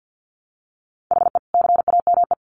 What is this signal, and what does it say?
Morse code, heard as a keyed tone, about 50 wpm, 700 Hz. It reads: HE CAME
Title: Sentences from Top 1400 Words 50wpm